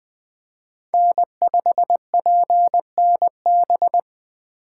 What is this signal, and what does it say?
Morse code, heard as a keyed tone, about 20 wpm, 700 Hz. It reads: N5PNB